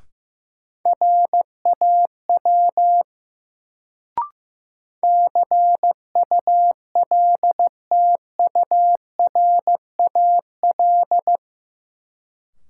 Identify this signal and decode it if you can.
Morse code, heard as a keyed tone, about 15 wpm, 700 Hz. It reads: RAW CULTURAL
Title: Sentences from Top 1500 Words - New Word Drill - 15wpm